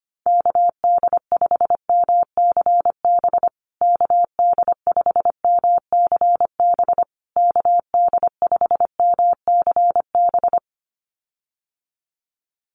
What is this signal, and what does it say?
Morse code, heard as a keyed tone, about 25 wpm, 700 Hz. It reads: KD5MCB KD5MCB KD5MCB
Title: USA Call Signs Repeated 3x 25wpm